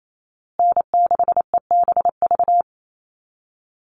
Morse code, heard as a keyed tone, about 28 wpm, 700 Hz. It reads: N6EBV
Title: USA Call Signs Repeated 3x 28wpm